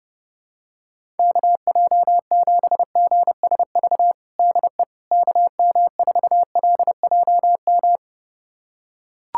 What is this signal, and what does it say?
Morse code, heard as a keyed tone, about 30 wpm, 700 Hz. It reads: KJ7GSV DE KM4LJM
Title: USA Callsign DE Callsign - 6 Characters - 30wpm